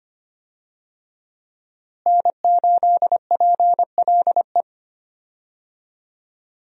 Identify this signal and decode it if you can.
Morse code, heard as a keyed tone, about 25 wpm, 700 Hz. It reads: N8PLE
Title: USA Call Signs - 5 Characters - Repeated 3x - 25wpm